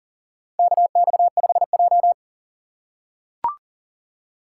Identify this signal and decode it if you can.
Morse code, heard as a keyed tone, about 40 wpm, 700 Hz. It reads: KX5J
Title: USA Call Signs - 3x Character Spacing - 40wpm